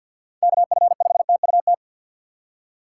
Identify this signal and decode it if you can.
Morse code, heard as a keyed tone, about 52 wpm, 690 Hz. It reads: KF5TUT
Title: USA Call Signs - 6 Characters - 3x Character Spacing - 50wpm